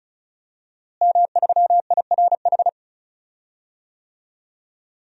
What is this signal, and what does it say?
Morse code, heard as a keyed tone, about 34 wpm, 700 Hz. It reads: M3IRH
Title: DXCC-100 Call Signs - 5 Characters - Repeated 2x - 35wpm